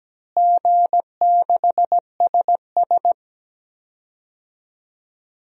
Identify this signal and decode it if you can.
Morse code, heard as a keyed tone, about 17 wpm, 700 Hz. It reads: G6SS
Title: DXCC-100 Call Signs - 4 Characters - Repeated 3x - 17wpm